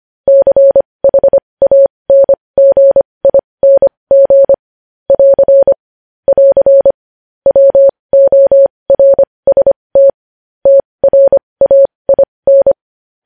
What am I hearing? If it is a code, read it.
Morse code, heard as a keyed tone, about 25 wpm, spaced as for 20 wpm, 550 Hz. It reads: CHANGING <AR> <AR> WORST TRAIN
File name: podcast / News headlines-1x-25x20